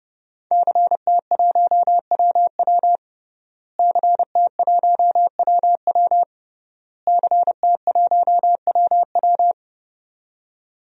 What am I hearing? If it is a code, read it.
Morse code, heard as a keyed tone, about 30 wpm, 700 Hz. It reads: CT1WW CT1WW CT1WW
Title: DXCC-100 Call Signs - 5 Characters - Repeated 3x - 30wpm